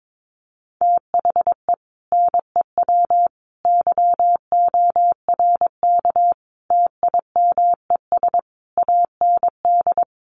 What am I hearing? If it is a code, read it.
Morse code, heard as a keyed tone, about 22 wpm, 700 Hz. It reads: THE NEW YORK TIMES AND
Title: Sets of 5 Words Encore 22wpm